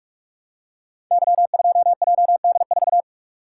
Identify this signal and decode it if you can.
Morse code, heard as a keyed tone, about 44 wpm, 700 Hz. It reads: Y2JD4
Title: Code Groups - 5 Letter:Numbers - 45wpm